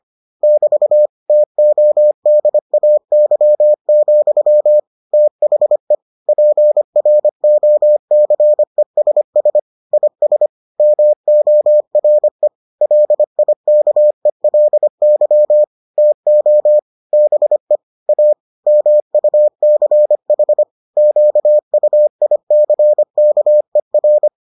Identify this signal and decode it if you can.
Morse code, heard as a keyed tone, about 25 wpm, 600 Hz. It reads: <BT> TODAY, THE PROCESS IS MORE LIKELY TO BE A MUCH QUICKER